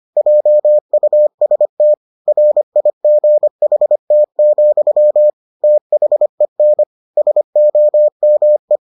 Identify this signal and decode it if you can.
Morse code, heard as a keyed tone, about 25 wpm, 600 Hz. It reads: JUST RIGHT, THEN SOME